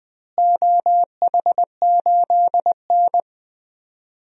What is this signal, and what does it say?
Morse code, heard as a keyed tone, about 20 wpm, 700 Hz. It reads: OH8N
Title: DXCC-100 Call Signs - 4 Characters - Repeated 3x - 20wpm